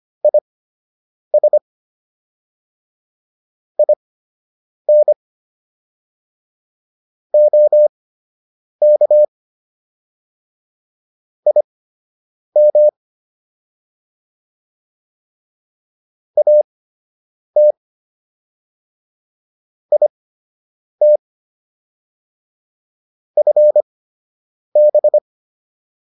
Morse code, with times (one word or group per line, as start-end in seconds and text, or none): 0.2-1.6 IS
3.8-5.1 IN
7.3-9.2 OK
11.5-12.9 IM
16.4-17.7 AT
19.9-21.2 IT
23.4-25.2 FB